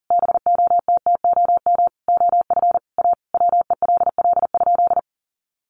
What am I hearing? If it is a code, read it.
Morse code, heard as a keyed tone, about 40 wpm, 700 Hz. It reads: BOTTOM OF A WELL?
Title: Sentences from Top 900 Words 40wpm